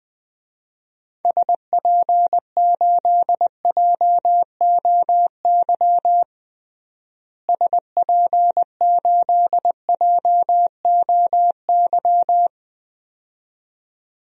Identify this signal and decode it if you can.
Morse code, heard as a keyed tone, about 20 wpm, 700 Hz. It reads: SP8JOY SP8JOY
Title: DXCC-100 Call Signs - 6 Characters - Repeated 2x - 20wpm